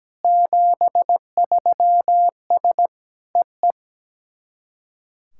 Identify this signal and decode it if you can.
Morse code, heard as a keyed tone, about 17 wpm, 700 Hz. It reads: 73S EE
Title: SOTA QSO Practice 17wpm